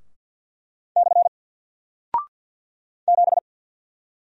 Morse code, heard as a keyed tone, about 48 wpm, 700 Hz. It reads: / 7
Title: Single Character (Letter:Number:Punctuation:Prosign) Alt 50wpm